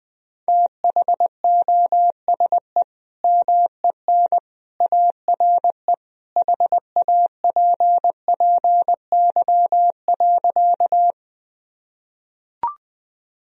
Morse code, heard as a keyed tone, about 20 wpm, 700 Hz. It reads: THOSE MEN ARE HAPPY.
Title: Sentences from Top 1000 Words - Review Part 1 - 20wpm